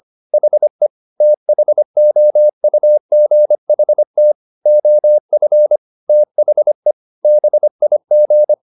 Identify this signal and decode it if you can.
Morse code, heard as a keyed tone, about 25 wpm, 600 Hz. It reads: HE THOUGHT OF THE BIG
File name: the old man and the sea part 1 25wpm